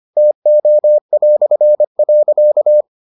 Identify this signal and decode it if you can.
Morse code, heard as a keyed tone, about 25 wpm, 600 Hz. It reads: TO".